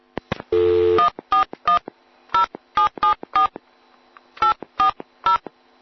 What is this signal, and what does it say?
DTMF, heard as telephone touch tones, tones of about 120 ms. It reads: 1410774 840